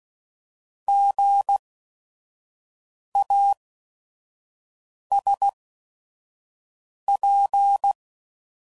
Morse code, heard as a keyed tone, about 16 wpm, spaced as for 5 wpm, 800 Hz. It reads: GASP
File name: Prefixes / By First Letter / Gs